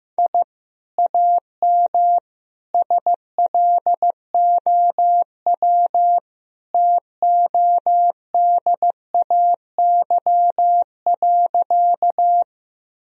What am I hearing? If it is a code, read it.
Morse code, heard as a keyed tone, about 16 wpm, 700 Hz. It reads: I AM SLOW TODAY.
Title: Sentences from Top 1000 Words - Review Part 2 - 15wpm